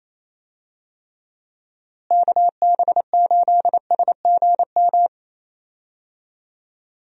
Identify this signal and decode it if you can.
Morse code, heard as a keyed tone, about 28 wpm, 700 Hz. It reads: KB8SGM
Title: USA Call Signs - 6 Characters - Repeated 3x - 28wpm